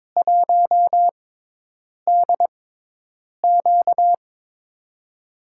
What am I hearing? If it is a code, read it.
Morse code, heard as a keyed tone, about 22 wpm, spaced as for 8 wpm, 700 Hz. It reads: 1DQ
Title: USA Call Signs - 4 Characters - 6x Character Spacing - 22wpm